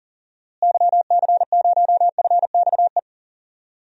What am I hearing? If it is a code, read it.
Morse code, heard as a keyed tone, about 40 wpm, 700 Hz. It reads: YC0FXE
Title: DXCC-100 Call Signs - 6 Characters - Repeated 2x - 40wpm